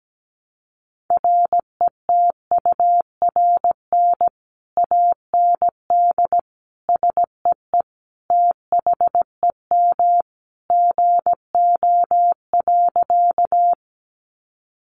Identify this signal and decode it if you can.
Morse code, heard as a keyed tone, about 17 wpm, 700 Hz. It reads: RETURN AND SEE THEM GO.
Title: Sentences from Top 800 Words 17wpm